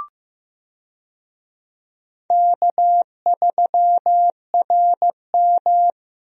Morse code, heard as a keyed tone, about 15 wpm, 700 Hz. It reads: K3RM
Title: USA Call Signs - 4 Characters - Repeated 2x - 15wpm